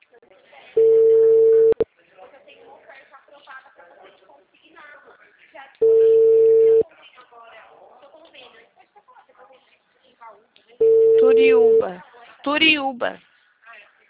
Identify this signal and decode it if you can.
Ringback tone (China, 450 Hz), with speech.
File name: out- -2005- - - .58090